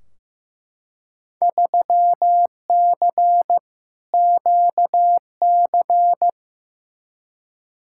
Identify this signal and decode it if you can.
Morse code, heard as a keyed tone, about 15 wpm, 700 Hz. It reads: 3C QC
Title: Field Day Exchanges Encore - 15wpm